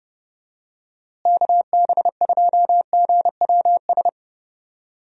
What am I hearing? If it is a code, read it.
Morse code, heard as a keyed tone, about 30 wpm, 700 Hz. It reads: KB2GWS
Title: USA Call Signs - 6 Characters - Repeated 2x - 30wpm